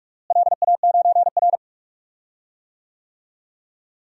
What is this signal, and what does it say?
Morse code, heard as a keyed tone, about 44 wpm, 700 Hz. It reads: LA9R
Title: DXCC-100 Call Signs - 4 Characters - Repeated 2x - 45wpm